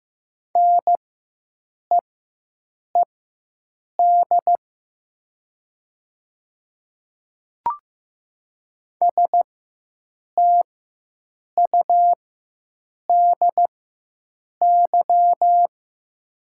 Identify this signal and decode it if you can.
Morse code, heard as a keyed tone, about 15 wpm, spaced as for 7 wpm, 700 Hz. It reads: NEED STUDY
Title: 4x Character Spacing - Top 500 Words - 15wpm